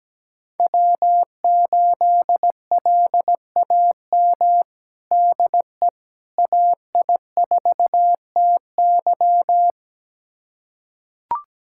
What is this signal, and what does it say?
Morse code, heard as a keyed tone, about 17 wpm, 700 Hz. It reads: W8LAM DE AI4TY
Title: USA Callsign DE Callsign - 5 Characters - 17wpm